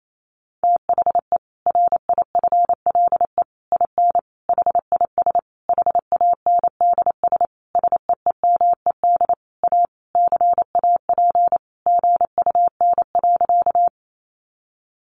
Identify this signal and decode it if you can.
Morse code, heard as a keyed tone, about 28 wpm, 700 Hz. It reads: THE RIFLE IN HIS HANDS SEEMED A CAP GUN.
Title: A Sound Of Thunder 28wpm